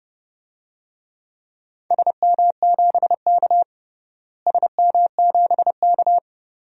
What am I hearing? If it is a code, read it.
Morse code, heard as a keyed tone, about 30 wpm, 700 Hz. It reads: SM7K SM7K
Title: DXCC-100 Call Signs - 4 Characters - Repeated 2x - 30wpm